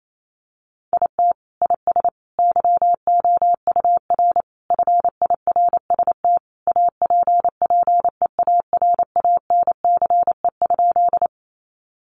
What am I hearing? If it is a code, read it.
Morse code, heard as a keyed tone, about 28 wpm, 700 Hz. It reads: IT IS YOUR FIRST APPEARANCE?
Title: Sentences from Top 1100 Words 28wpm